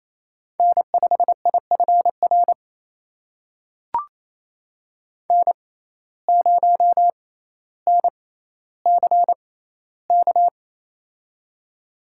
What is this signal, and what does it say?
Morse code, heard as a keyed tone, about 28 wpm, spaced as for 14 wpm, 700 Hz. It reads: N5IFR N0NCK
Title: USA Call Signs - 6x Character Spacing - 28wpm